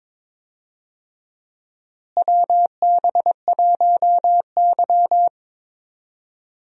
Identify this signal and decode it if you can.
Morse code, heard as a keyed tone, about 22 wpm, 700 Hz. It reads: WB1Y
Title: USA Call Signs - 4 Characters - Repeated 2x - 22wpm